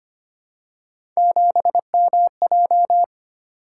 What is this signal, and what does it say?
Morse code, heard as a keyed tone, about 25 wpm, 700 Hz. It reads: 7MJ
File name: Three Letter-Number Combinations - 25wpm